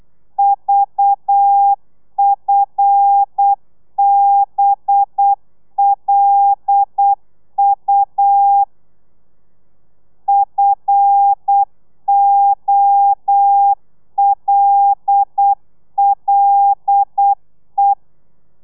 Morse code, heard as a keyed tone, about 8 wpm, 800 Hz. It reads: VFBLU FOLLE
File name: telegrafia / i2tfj / Lez5 35